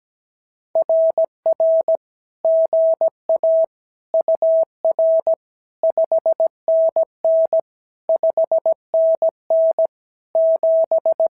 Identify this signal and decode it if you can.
Morse code, heard as a keyed tone, about 17 wpm, 650 Hz. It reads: RR GA UR 5NN 5NN 7
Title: SOTA QSO Practice 17wpm